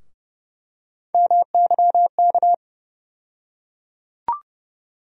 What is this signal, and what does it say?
Morse code, heard as a keyed tone, about 30 wpm, 700 Hz. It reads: MYK
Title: USA Call Sign Postfixes - 3 Letters 30wpm